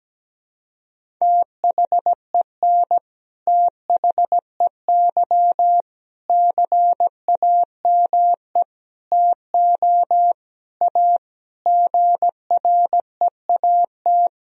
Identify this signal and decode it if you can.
Morse code, heard as a keyed tone, about 17 wpm, 700 Hz. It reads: THEN THEY CAME TO A GREAT
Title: Sentences from Top 1000 Words - Review Part 1 - 17wpm